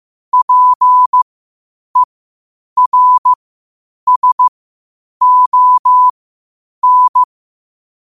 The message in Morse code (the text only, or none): PERSON